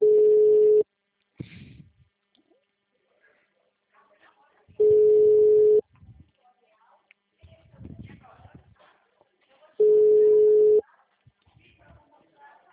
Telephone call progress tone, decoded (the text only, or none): ringback tone